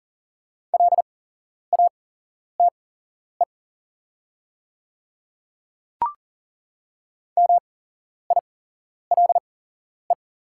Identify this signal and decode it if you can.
Morse code, heard as a keyed tone, about 40 wpm, spaced as for 11 wpm, 700 Hz. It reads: LATE MILE
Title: 8x Character Spacing - Top 500 Words - 40wpm